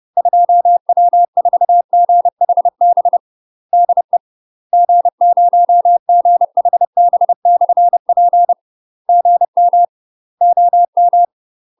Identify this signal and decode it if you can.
Morse code, heard as a keyed tone, about 30 wpm, 700 Hz. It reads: 2W4GHB DE G0GHB/P GM OM
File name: UK MASTER QSOs TRAINING/30WPM